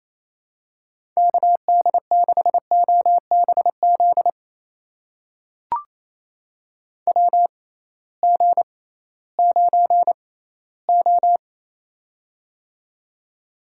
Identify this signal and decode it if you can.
Morse code, heard as a keyed tone, about 28 wpm, spaced as for 16 wpm, 700 Hz. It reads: KD6OBZ WG9O
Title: USA Call Signs - 6x Character Spacing - 28wpm